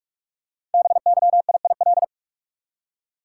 Morse code, heard as a keyed tone, about 44 wpm, 700 Hz. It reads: BYIIL
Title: Code Groups - 5 Letters - 45wpm